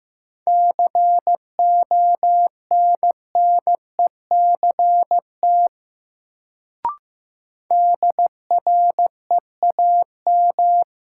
Morse code, heard as a keyed tone, about 15 wpm, 700 Hz. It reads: CONNECT DREAM
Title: Sentences from Top 1500 Words - New Word Drill - 15wpm